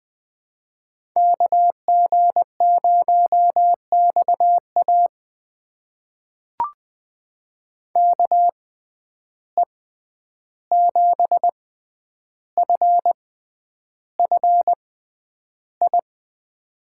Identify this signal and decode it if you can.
Morse code, heard as a keyed tone, about 20 wpm, spaced as for 10 wpm, 700 Hz. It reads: KG0XA KE7FFI
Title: USA Call Signs - 6x Character Spacing - 20wpm